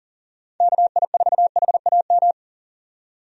Morse code, heard as a keyed tone, about 40 wpm, 700 Hz. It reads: KI4HAM
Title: USA Call Signs - 6 Characters - 40wpm